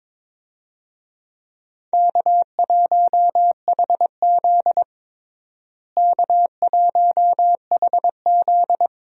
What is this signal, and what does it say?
Morse code, heard as a keyed tone, about 22 wpm, 700 Hz. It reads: K1HZ K1HZ
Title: USA Call Signs - 4 Characters - Repeated 2x - 22wpm